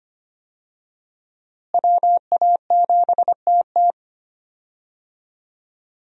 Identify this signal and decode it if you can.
Morse code, heard as a keyed tone, about 25 wpm, 700 Hz. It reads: WA7TT
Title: USA Call Signs - 5 Characters - Repeated 3x - 25wpm